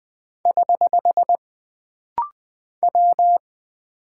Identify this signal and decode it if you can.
Morse code, heard as a keyed tone, about 20 wpm, 700 Hz. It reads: <HH> W
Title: Single Character (Letter:Number:Punctuation:Prosign) - Rapid-Fire - 20wpm